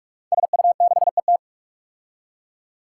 Morse code, heard as a keyed tone, about 44 wpm, 700 Hz. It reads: SU6ET
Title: Five Challenging Letter-Number Combinations 45wpm